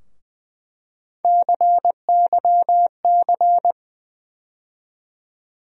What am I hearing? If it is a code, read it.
Morse code, heard as a keyed tone, about 20 wpm, 700 Hz. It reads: CYC
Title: USA Call Sign Postfixes - 3 Letters 20wpm